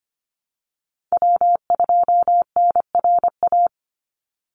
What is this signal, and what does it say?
Morse code, heard as a keyed tone, about 25 wpm, 700 Hz. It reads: W2NRA
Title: Call Signs Repeated 2x - 25wpm